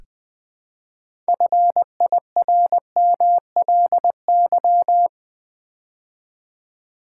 Morse code, heard as a keyed tone, about 20 wpm, 700 Hz. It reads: FIRMLY
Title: Build A Word - 2 Letters At A Time - 6-Letter Words - 20wpm